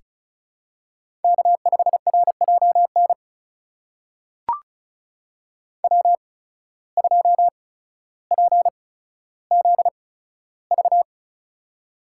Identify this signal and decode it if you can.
Morse code, heard as a keyed tone, about 36 wpm, spaced as for 15 wpm, 700 Hz. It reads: K5RJN W2PZV
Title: USA Call Signs - 5 Characters - 8x Character Spacing - 35wpm